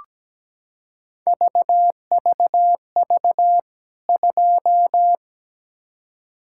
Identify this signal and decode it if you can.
Morse code, heard as a keyed tone, about 17 wpm, 700 Hz. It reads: VVV 2
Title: K1USN SST QSO Practice 17wpm